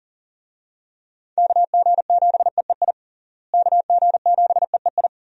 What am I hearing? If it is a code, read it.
Morse code, heard as a keyed tone, about 37 wpm, 700 Hz. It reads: KG7EEI KG7EEI
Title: USA Call Signs - 6 Characters - Repeated 2x - 40wpm